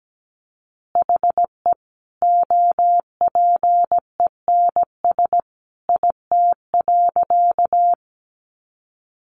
Morse code, heard as a keyed tone, about 17 wpm, 700 Hz. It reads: HE OPENS IT.
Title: Additional Easy Phrases 17wpm